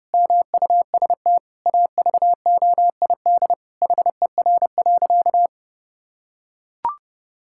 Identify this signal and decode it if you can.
Morse code, heard as a keyed tone, about 32 wpm, 700 Hz. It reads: MUST AVOID HER.
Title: Sentences from Top 1000 Words - Review Part 6 - 30wpm